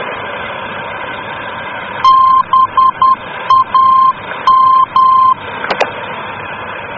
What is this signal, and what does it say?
Morse code, heard as a keyed tone, about 10 wpm, 1050 Hz, with noise. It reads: BAM
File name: germany / mx / BAM-D